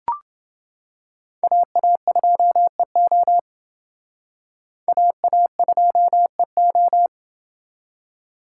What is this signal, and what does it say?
Morse code, heard as a keyed tone, about 28 wpm, 700 Hz. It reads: AA2EO AA2EO